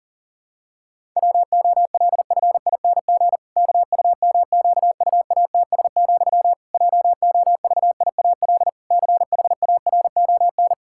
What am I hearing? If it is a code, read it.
Morse code, heard as a keyed tone, about 40 wpm, 700 Hz. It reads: WOLFING KUMQUATS, JOVIAL CHARON
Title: Pangram Challenge v2 - 40wpm